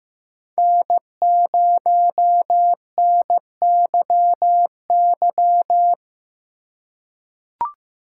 Morse code, heard as a keyed tone, about 15 wpm, 700 Hz. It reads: N0NYY
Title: USA Call Signs - 5 Characters - 3x Character Spacing - 15wpm